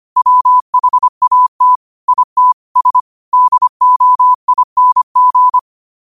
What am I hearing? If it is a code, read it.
Morse code, heard as a keyed tone, about 25 wpm, 1000 Hz. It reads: WHAT IT S DOING